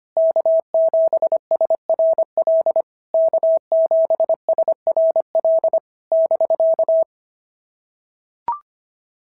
Morse code, heard as a keyed tone, about 22 wpm, 650 Hz. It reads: K7SRL K7SRL <BK>
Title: SOTA QSO Practice 25wpm